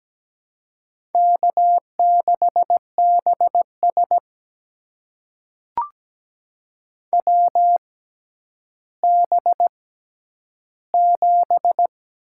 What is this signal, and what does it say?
Morse code, heard as a keyed tone, about 17 wpm, spaced as for 10 wpm, 700 Hz. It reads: K6BS WB7
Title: USA Call Signs - 6x Character Spacing - 17wpm